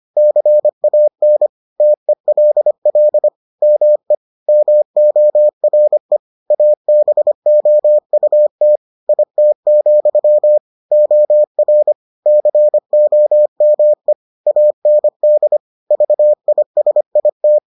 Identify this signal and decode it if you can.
Morse code, heard as a keyed tone, about 25 wpm, 600 Hz. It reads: CAN TELL ME MORE ABOUT IT, OR COME AND VISIT